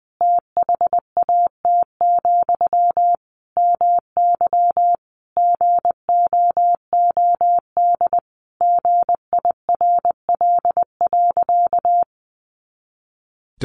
Morse code, heard as a keyed tone, about 20 wpm, 700 Hz. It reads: THAT, MY GOOD GIRL.
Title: Sentences from Top 400 Words 20wpm